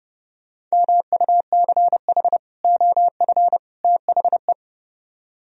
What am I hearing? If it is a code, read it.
Morse code, heard as a keyed tone, about 33 wpm, 700 Hz. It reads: MUCH OF THE